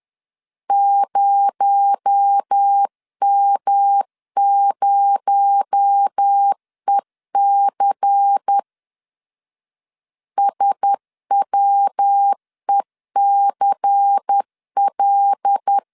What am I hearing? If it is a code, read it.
Morse code, heard as a keyed tone, about 10 wpm, 800 Hz. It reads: 0M0EC SWECL
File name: LEZIONI / 6 PKX0 / 6 PKX0 10wpm 2707